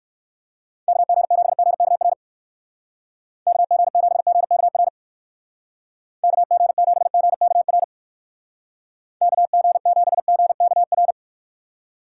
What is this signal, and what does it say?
Morse code, heard as a keyed tone, about 55 wpm, 700 Hz. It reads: KG7GKR KG7GKR KG7GKR KG7GKR
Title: Speed Racing - USA 2x3 Call Signs 45wpm